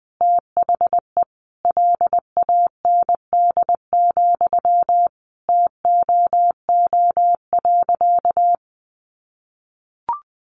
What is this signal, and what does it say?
Morse code, heard as a keyed tone, about 20 wpm, 700 Hz. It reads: THE LAND, TOO.
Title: Sentences from Top 1500 Words 20wpm